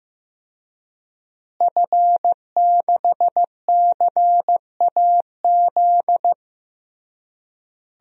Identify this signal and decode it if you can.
Morse code, heard as a keyed tone, about 15 wpm, 700 Hz. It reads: F6CAZ
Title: DXCC-100 Call Signs - 5 Characters - Repeated 3x - 15wpm